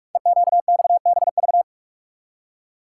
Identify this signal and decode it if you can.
Morse code, heard as a keyed tone, about 44 wpm, 700 Hz. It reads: EQXBV
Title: Code Groups - 5 Letter:Numbers - 45wpm